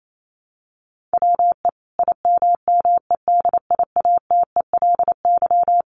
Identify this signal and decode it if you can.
Morse code, heard as a keyed tone, about 28 wpm, 700 Hz. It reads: WE IMMEDIATELY
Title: Sentences from Top 1100 Words 28wpm